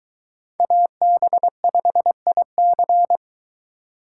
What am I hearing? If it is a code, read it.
Morse code, heard as a keyed tone, about 22 wpm, 700 Hz. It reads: AB5IC